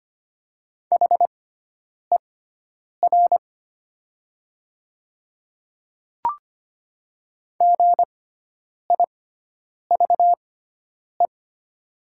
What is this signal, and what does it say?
Morse code, heard as a keyed tone, about 25 wpm, spaced as for 9 wpm, 700 Hz. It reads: HER GIVE
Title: 6x Character Spacing - Top 200 Words - 25wpm